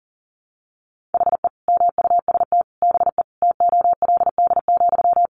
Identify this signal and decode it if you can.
Morse code, heard as a keyed tone, about 40 wpm, 700 Hz. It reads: HE MUST BE TOLD,
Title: Sentences from Top 1400 Words 40wpm